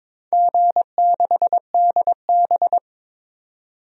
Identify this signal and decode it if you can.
Morse code, heard as a keyed tone, about 22 wpm, 700 Hz. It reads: G6DB
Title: DXCC-100 Call Signs - 4 Characters - 22wpm